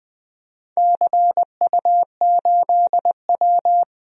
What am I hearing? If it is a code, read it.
Morse code, heard as a keyed tone, about 21 wpm, 700 Hz. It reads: CU8W
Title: DXCC-100 Call Signs - 4 Characters - Repeated 3x - 20wpm